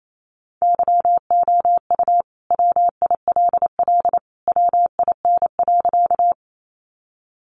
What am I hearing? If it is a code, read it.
Morse code, heard as a keyed tone, about 28 wpm, 700 Hz. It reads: YOU WILL WIN.
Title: Sentences from Top 1500 Words 28wpm